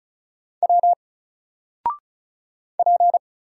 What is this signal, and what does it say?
Morse code, heard as a keyed tone, about 34 wpm, 700 Hz. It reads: W P
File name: Single-Letter-Number-35wpm